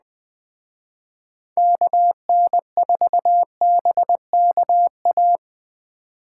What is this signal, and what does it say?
Morse code, heard as a keyed tone, about 21 wpm, 700 Hz. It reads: KN4BKA
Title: Speed Racing - USA 2x3 Call Signs 17wpm